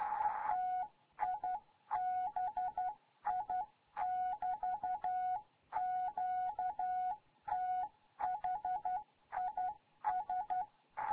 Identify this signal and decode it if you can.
Morse code, heard as a keyed tone, about 12 wpm, 740 Hz. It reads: TIBI <BT> QTH IS
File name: Morze-HA2MN